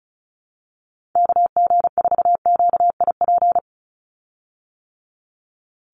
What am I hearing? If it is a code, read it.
Morse code, heard as a keyed tone, about 34 wpm, 700 Hz. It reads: KG4QIP
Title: USA Call Signs 35wpm